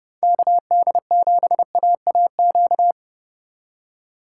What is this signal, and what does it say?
Morse code, heard as a keyed tone, about 29 wpm, 700 Hz. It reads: KD7AAQ
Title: USA Call Signs - 6x Character Spacing - 30wpm